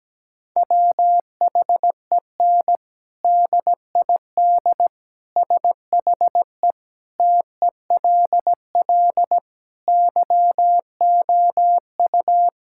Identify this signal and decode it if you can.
Morse code, heard as a keyed tone, about 17 wpm, 700 Hz. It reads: WHEN DID SHE TELL YOU
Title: Sentences from Top 1000 Words - Review Part 6 - 17wpm